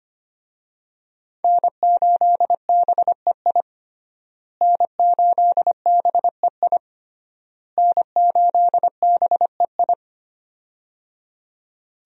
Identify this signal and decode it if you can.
Morse code, heard as a keyed tone, about 25 wpm, 700 Hz. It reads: N8BEI N8BEI N8BEI
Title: USA Call Signs - 5 Characters - Repeated 3x - 25wpm